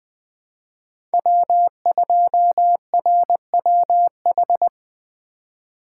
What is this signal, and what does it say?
Morse code, heard as a keyed tone, about 20 wpm, 700 Hz. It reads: W2RWH
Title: USA Call Signs - 5 Characters - 2x Character Spacing - 20wpm